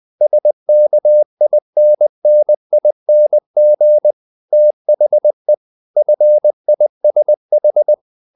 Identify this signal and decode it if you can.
Morse code, heard as a keyed tone, about 20 wpm, 600 Hz. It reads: SKINNING THE FISH
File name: the old man and the sea part 2 20wpm